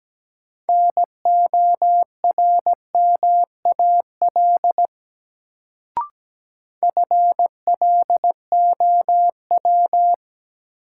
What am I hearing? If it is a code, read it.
Morse code, heard as a keyed tone, about 17 wpm, 700 Hz. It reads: NORMAL FLOW
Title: Sentences from Top 1200 Words - New Word Drill - 17wpm